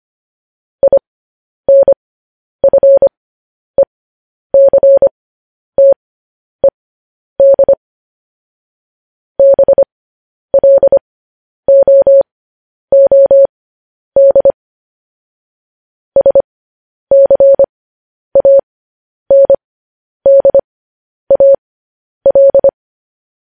Morse code, heard as a keyed tone, about 25 wpm, spaced as for 10 wpm, 550 Hz. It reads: INFECTED BLOOD SCANDAL
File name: podcast / News headlines-1x-25x10